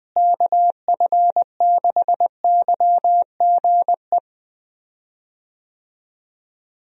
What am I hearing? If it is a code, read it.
Morse code, heard as a keyed tone, about 20 wpm, 700 Hz. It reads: KF6YGE
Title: USA Call Signs - 6 Characters - Repeated 2x - 20wpm